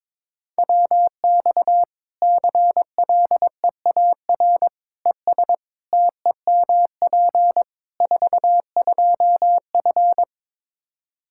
Morse code, heard as a keyed tone, about 22 wpm, 700 Hz. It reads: WX CLEAR ES TEMP 42F
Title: Practice QSO - Exchange 2 - Info - 22wpm